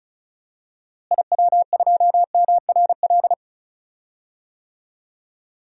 Morse code, heard as a keyed tone, about 36 wpm, 700 Hz. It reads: IW2MRL
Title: DXCC-100 Call Signs - 6 Characters - Repeated 3x - 35wpm